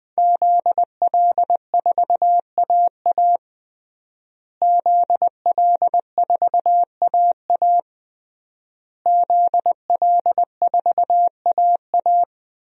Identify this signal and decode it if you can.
Morse code, heard as a keyed tone, about 21 wpm, 700 Hz. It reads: ZL4AA ZL4AA ZL4AA
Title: DXCC-100 Call Signs - 5 Characters - Repeated 3x - 20wpm